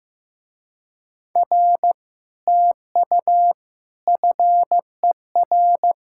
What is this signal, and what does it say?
Morse code, heard as a keyed tone, about 15 wpm, 700 Hz. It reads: R TU FER
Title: Speed-Racing Common QSO Phrases Encore - 15wpm